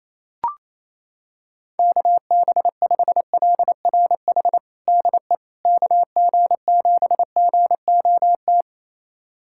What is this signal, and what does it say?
Morse code, heard as a keyed tone, about 28 wpm, 700 Hz. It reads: KB5LRH DE KG7GOT
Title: USA Callsign DE Callsign 28wpm